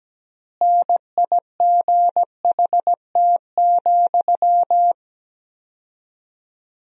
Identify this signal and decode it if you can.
Morse code, heard as a keyed tone, about 17 wpm, 700 Hz. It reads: NIGHT,
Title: 4x Word-Spacing - Sentences from Top 300 Words - 17wpm